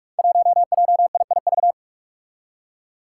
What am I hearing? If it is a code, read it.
Morse code, heard as a keyed tone, about 44 wpm, 700 Hz. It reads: 1JIIV
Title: Five Challenging Letter-Number Combinations 45wpm